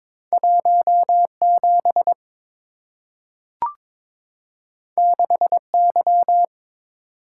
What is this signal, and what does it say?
Morse code, heard as a keyed tone, about 22 wpm, 690 Hz. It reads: 17 6Y
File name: practice / 22wpm / Two Letter-Number Combinations - 22wpm